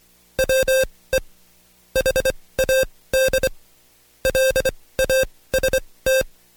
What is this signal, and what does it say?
Morse code, heard as a keyed tone, about 24 wpm, spaced as for 18 wpm, 530 Hz. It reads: WE HAD LAST
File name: Code Quick / AudioDownld2 / Seg135-Stepper8 17WPM